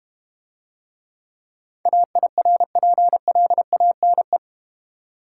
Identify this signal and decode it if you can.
Morse code, heard as a keyed tone, about 31 wpm, 700 Hz. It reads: AIRPLANE
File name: Easy Words - Speed-Racing with Hints - 25wpm